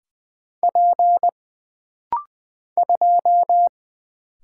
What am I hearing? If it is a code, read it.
Morse code, heard as a keyed tone, about 20 wpm, 700 Hz. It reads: P 2
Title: Single Letter-Number - Rapid-Fire 20wpm